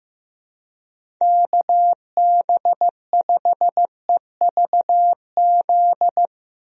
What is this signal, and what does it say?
Morse code, heard as a keyed tone, about 15 wpm, 700 Hz. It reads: KB5EVZ
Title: USA Call Signs - 6 Characters - Repeated 2x - 15wpm